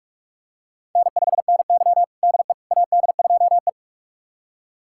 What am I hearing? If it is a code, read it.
Morse code, heard as a keyed tone, about 42 wpm, 700 Hz. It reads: N5NY DE AD2E
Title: USA Callsign DE Callsign - 4 Characters - 45wpm